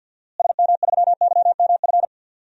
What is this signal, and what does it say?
Morse code, heard as a keyed tone, about 49 wpm, 700 Hz. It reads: SM3YMF
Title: DXCC-100 Call Signs - 6 Characters - Repeated 2x - 50wpm